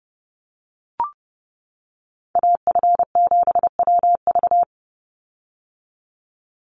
Morse code, heard as a keyed tone, about 28 wpm, 700 Hz. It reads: AF7WV
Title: USA Call Signs 30wpm